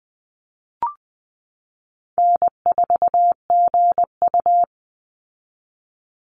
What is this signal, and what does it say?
Morse code, heard as a keyed tone, about 20 wpm, 700 Hz. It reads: N4GU
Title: Easy Call Signs - 20wpm